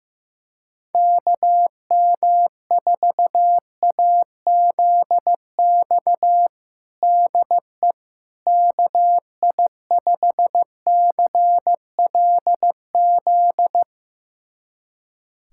Morse code, heard as a keyed tone, about 16 wpm, 700 Hz. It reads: KM4AZX DE KI5CLZ
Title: USA Callsign DE Callsign - 6 Characters - 15wpm